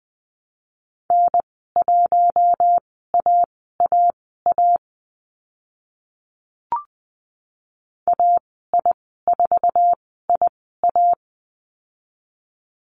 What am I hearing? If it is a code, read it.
Morse code, heard as a keyed tone, about 20 wpm, spaced as for 14 wpm, 700 Hz. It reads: N1AAA AI4IA
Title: 069 Lesson 12g - Letter d - Call Signs 20wpm